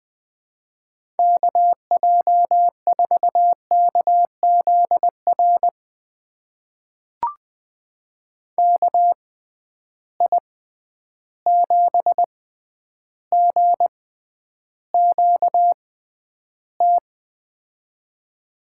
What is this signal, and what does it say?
Morse code, heard as a keyed tone, about 21 wpm, spaced as for 10 wpm, 700 Hz. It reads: KJ4KZR KI7GQT
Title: USA Call Signs - 6 Characters - 6x Character Spacing - 20wpm